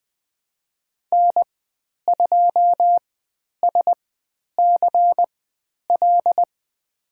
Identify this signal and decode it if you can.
Morse code, heard as a keyed tone, about 20 wpm, spaced as for 10 wpm, 700 Hz. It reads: N2SCL
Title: 095 Lesson 16g - Letter c - Call Signs 20wpm